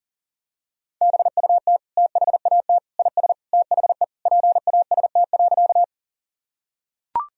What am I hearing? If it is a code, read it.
Morse code, heard as a keyed tone, about 40 wpm, 700 Hz. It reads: BUT THAT IS THE PAST.
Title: Sentences from Top 1000 Words - Review Part 3 - 40wpm